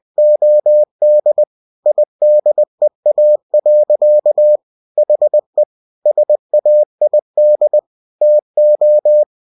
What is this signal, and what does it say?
Morse code, heard as a keyed tone, about 20 wpm, 600 Hz. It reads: OD IDEA. HE SAID TO
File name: cw-books / winnie-the-pooh / winnie-20wpm-page-0095